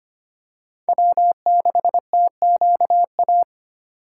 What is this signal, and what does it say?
Morse code, heard as a keyed tone, about 25 wpm, 700 Hz. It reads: W6TQA
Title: USA Call Signs - 4x Character Spacing - 25wpm